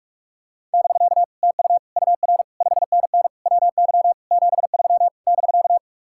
Morse code, heard as a keyed tone, about 50 wpm, 700 Hz. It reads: <BK> TU UR 5NN WY 73 <BK>